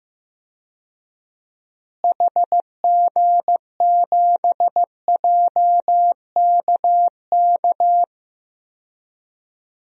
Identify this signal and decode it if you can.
Morse code, heard as a keyed tone, about 15 wpm, 700 Hz. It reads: HG7JKK
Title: DXCC-100 Call Signs - 6 Characters - Repeated 3x - 15wpm